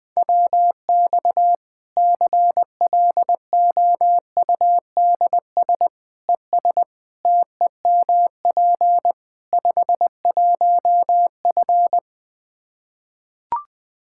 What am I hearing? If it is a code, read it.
Morse code, heard as a keyed tone, about 20 wpm, 700 Hz. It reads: WX CLOUDS ES TEMP 51F
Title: Practice QSO - Exchange 2 - Info - 20wpm